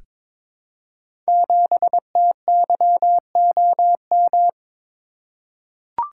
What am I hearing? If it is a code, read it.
Morse code, heard as a keyed tone, about 22 wpm, 700 Hz. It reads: 7TYOM
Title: Code Groups - 5 Letter:Numbers - 22wpm